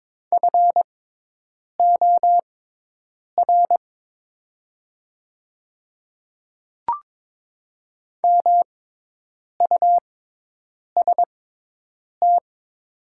Morse code, heard as a keyed tone, about 22 wpm, spaced as for 8 wpm, 700 Hz. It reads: FOR MUST
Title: 6x Character Spacing - Top 300 Words - 22wpm